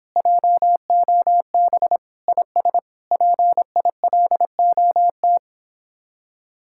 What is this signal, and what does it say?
Morse code, heard as a keyed tone, about 26 wpm, 700 Hz. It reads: JOB IS PILOT